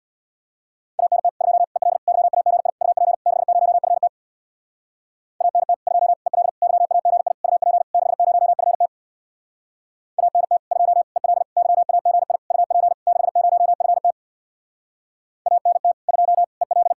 Morse code, heard as a keyed tone, about 64 wpm, 700 Hz. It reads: ANT 2 EL YAGI UP 60FT ANT 2 EL YAGI UP 60FT ANT 2 EL YAGI UP 60FT ANT 2 EL